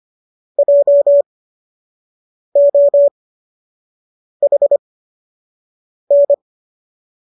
Morse code, heard as a keyed tone, about 25 wpm, spaced as for 6 wpm, 570 Hz. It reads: JOHN